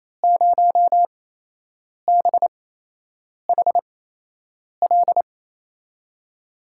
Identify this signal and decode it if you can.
Morse code, heard as a keyed tone, about 27 wpm, spaced as for 8 wpm, 700 Hz. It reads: 0BHL
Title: USA Call Signs - 6 Characters - 8x Character Spacing - 28wpm